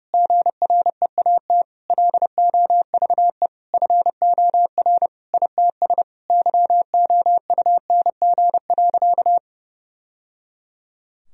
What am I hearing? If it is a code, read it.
Morse code, heard as a keyed tone, about 32 wpm, 700 Hz. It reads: GREAT LOVE FOR ITS YOUNG.
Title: Sentences from Top 1000 Words - Review Part 2 - 30wpm